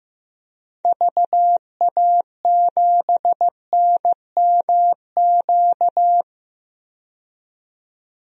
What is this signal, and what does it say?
Morse code, heard as a keyed tone, about 15 wpm, 690 Hz. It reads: VA7NMQ
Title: DXCC-100 Call Signs - 6 Characters - 15wpm